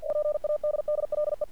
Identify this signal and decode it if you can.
Morse code, heard as a keyed tone, about 47 wpm, 600 Hz. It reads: PADDLE
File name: archived / CRSnet